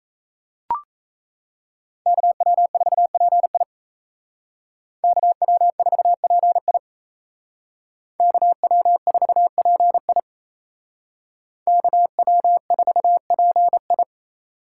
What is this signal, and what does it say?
Morse code, heard as a keyed tone, about 35 wpm, 700 Hz. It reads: KW4PI KW4PI KW4PI KW4PI